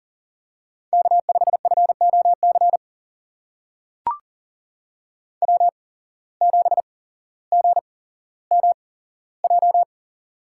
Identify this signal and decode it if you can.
Morse code, heard as a keyed tone, about 40 wpm, spaced as for 17 wpm, 700 Hz. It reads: K5FOC W7GMJ
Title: USA Call Signs - 5 Characters - 8x Character Spacing - 40wpm